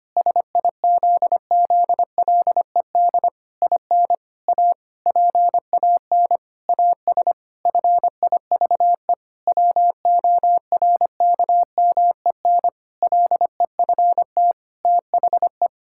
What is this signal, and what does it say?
Morse code, heard as a keyed tone, about 25 wpm, 700 Hz. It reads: SIZZLED IN A PAN AS FIVE WORKMEN LEFT THE
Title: Pangram Challenge v2 - 25wpm